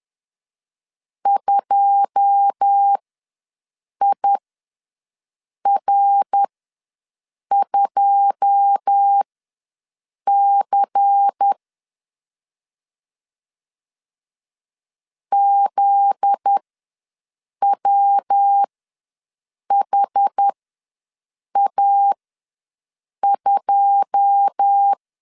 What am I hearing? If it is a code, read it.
Morse code, heard as a keyed tone, about 10 wpm, spaced as for 6 wpm, 800 Hz. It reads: 2IR2C ZWHA2
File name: LEZIONI / 9 2 ws 10wpm 4130